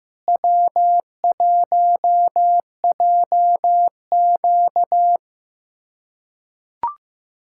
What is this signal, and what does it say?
Morse code, heard as a keyed tone, about 15 wpm, 700 Hz. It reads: W1JQ
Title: USA Callsign DE Callsign - 4 Characters - 15wpm